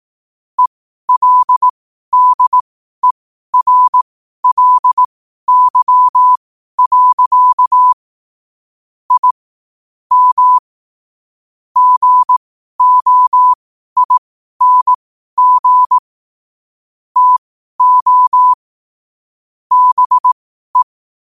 Morse code, heard as a keyed tone, about 18 wpm, spaced as for 12 wpm, 1000 Hz. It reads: ELDERLY. I M GOING TO BE
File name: qotdfiles / 12wpm